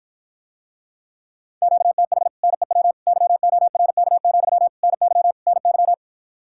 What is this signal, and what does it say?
Morse code, heard as a keyed tone, about 54 wpm, 700 Hz. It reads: QTH NEW YORK, NY NY